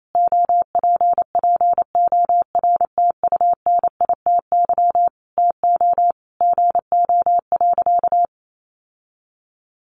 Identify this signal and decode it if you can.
Morse code, heard as a keyed tone, about 28 wpm, 700 Hz. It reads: OPPORTUNITY TO GO.
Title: Sentences from Top 1000 Words 28wpm